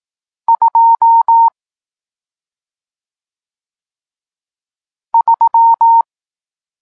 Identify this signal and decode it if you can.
Morse code, heard as a keyed tone, about 18 wpm, 920 Hz. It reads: 2 3
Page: Tekst 40 – Nauka znaków przestankowych